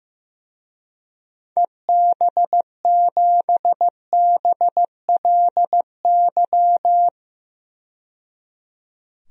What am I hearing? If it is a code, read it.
Morse code, heard as a keyed tone, about 15 wpm, 700 Hz. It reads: EB7BLY
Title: DXCC-100 Call Signs - 6 Characters - 15wpm